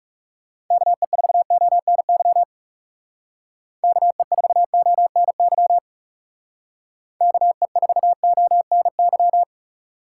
Morse code, heard as a keyed tone, about 37 wpm, 700 Hz. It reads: KE4ONY KE4ONY KE4ONY
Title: Speed Racing - USA 2x3 Call Signs 30wpm